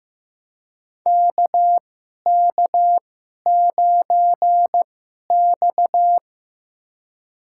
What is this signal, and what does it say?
Morse code, heard as a keyed tone, about 15 wpm, spaced as for 11 wpm, 700 Hz. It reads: KK9X
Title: USA Call Signs - 2x Character Spacing - 15wpm